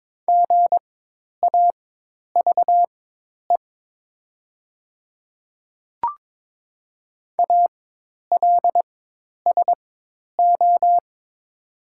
Morse code, heard as a keyed tone, about 22 wpm, spaced as for 10 wpm, 700 Hz. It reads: GAVE ALSO
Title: 4x Character Spacing - Top 500 Words - 22wpm